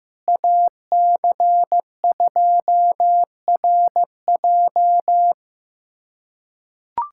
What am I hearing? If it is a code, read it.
Morse code, heard as a keyed tone, about 15 wpm, 700 Hz. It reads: AC2RJ
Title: K1USN SST QSO Practice - 15wpm